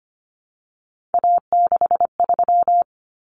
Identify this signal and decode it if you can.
Morse code, heard as a keyed tone, about 25 wpm, 700 Hz. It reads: A63
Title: Common Cut Numbers - 3-Digit 25wpm